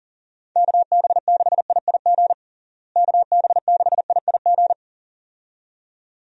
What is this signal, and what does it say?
Morse code, heard as a keyed tone, about 40 wpm, 700 Hz. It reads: KB6IIG KB6IIG
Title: USA Call Signs - 6 Characters - Repeated 2x - 40wpm